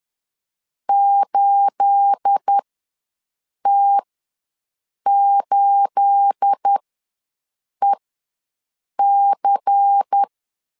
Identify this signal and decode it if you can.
Morse code, heard as a keyed tone, about 10 wpm, spaced as for 6 wpm, 800 Hz. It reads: 8T8EC
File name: LEZIONI / 9 28 ws 10wpm 4124